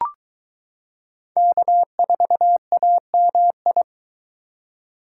Morse code, heard as a keyed tone, about 22 wpm, 700 Hz. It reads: K4AMI